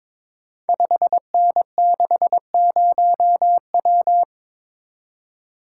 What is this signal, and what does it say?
Morse code, heard as a keyed tone, about 22 wpm, 700 Hz. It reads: 5N60W
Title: Code Groups - 5 Letter:Numbers - 22wpm